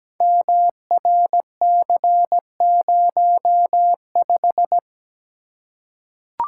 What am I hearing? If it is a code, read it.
Morse code, heard as a keyed tone, about 17 wpm, 700 Hz. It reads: MRC05
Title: Code Groups - 5 Letter:Numbers - 17wpm